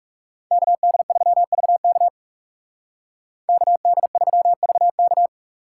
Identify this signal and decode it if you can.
Morse code, heard as a keyed tone, about 36 wpm, 700 Hz. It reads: KD3VK KD3VK
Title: Speed Racing - USA 2x2 Call Signs 30wpm